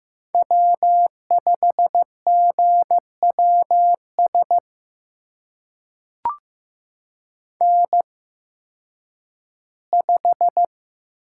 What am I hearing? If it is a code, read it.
Morse code, heard as a keyed tone, about 15 wpm, spaced as for 10 wpm, 700 Hz. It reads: W5GWS N5
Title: USA Call Signs - 5 Characters - 8x Character Spacing - 15wpm